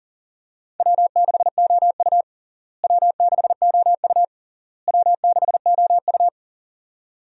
Morse code, heard as a keyed tone, about 40 wpm, 700 Hz. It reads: W6OU W6OU W6OU
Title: USA Call Signs - 4 Characters - Repeated 3x - 40wpm